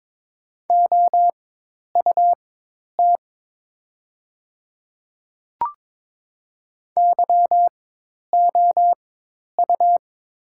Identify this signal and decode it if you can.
Morse code, heard as a keyed tone, about 22 wpm, spaced as for 10 wpm, 700 Hz. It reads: OUT YOU
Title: 4x Character Spacing - Top 100 Words - 22wpm